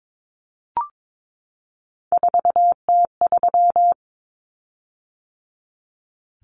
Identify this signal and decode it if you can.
Morse code, heard as a keyed tone, about 22 wpm, 700 Hz. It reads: 4T3
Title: Common Cut Numbers - 3-Digit 22wpm